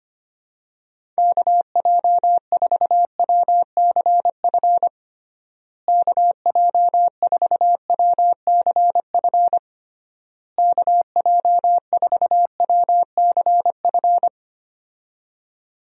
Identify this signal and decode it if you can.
Morse code, heard as a keyed tone, about 25 wpm, 700 Hz. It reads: KJ4WCF KJ4WCF KJ4WCF
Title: USA Call Signs - 6 Characters - Repeated 3x - 25wpm